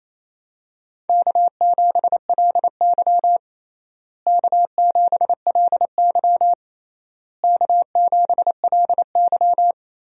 Morse code, heard as a keyed tone, about 28 wpm, 700 Hz. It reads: K7LY K7LY K7LY
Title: USA Call Signs - 4 Characters - Repeated 3x - 28wpm